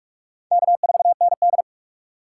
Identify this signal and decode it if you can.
Morse code, heard as a keyed tone, about 46 wpm, 700 Hz. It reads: K4ND
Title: USA Call Signs - 4 Characters - 3x Character Spacing - 45wpm